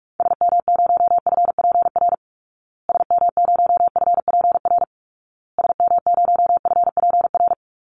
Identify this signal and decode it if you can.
Morse code, heard as a keyed tone, about 44 wpm, 700 Hz. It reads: SM0FPR SM0FPR SM0FPR
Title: Call Signs Repeated 2x - 45wpm